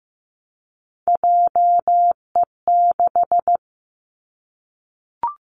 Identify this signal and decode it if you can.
Morse code, heard as a keyed tone, about 15 wpm, 700 Hz. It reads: JE6
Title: Three Challenging Letter-Number Combinations 15wpm